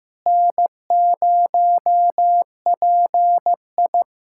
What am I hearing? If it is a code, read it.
Morse code, heard as a keyed tone, about 15 wpm, 700 Hz. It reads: N0PI
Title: USA Callsign DE Callsign - 4 Characters - 15wpm